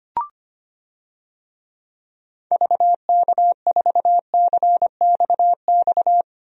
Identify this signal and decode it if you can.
Morse code, heard as a keyed tone, about 25 wpm, 700 Hz. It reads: VK4CXX
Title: DXCC-100 Call Signs - 6 Characters - 25wpm